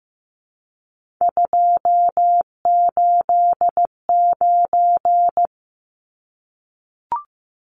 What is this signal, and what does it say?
Morse code, heard as a keyed tone, about 15 wpm, 700 Hz. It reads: 289
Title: Numbers - 3-Digit 15wpm